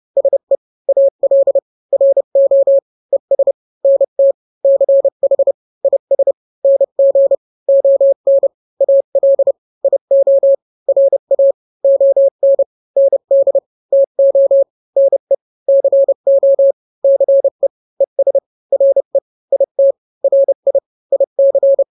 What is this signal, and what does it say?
Morse code, heard as a keyed tone, about 30 wpm, 550 Hz. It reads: SE AL RO ES NT CH IS NG ON AL IO RA ON ND TO NE CO CE ES RE IT RI IC
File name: n-grams / 2-grams 30 25 WPM